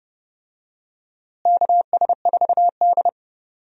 Morse code, heard as a keyed tone, about 30 wpm, 700 Hz. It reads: KS4D
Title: USA Call Signs - 4 Characters - Repeated 3x - 30wpm